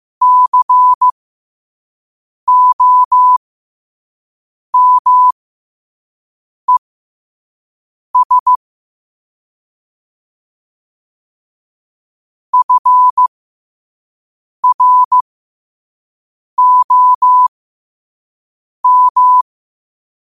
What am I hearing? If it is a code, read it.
Morse code, heard as a keyed tone, about 15 wpm, spaced as for 5 wpm, 1000 Hz. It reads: COMES FROM
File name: qotdfiles / 5wpm